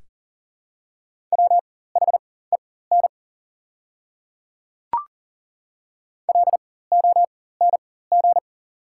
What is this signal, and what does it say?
Morse code, heard as a keyed tone, about 40 wpm, spaced as for 19 wpm, 700 Hz. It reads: WHEN LONG
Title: 4x Character Spacing - Top 100 Words - 40wpm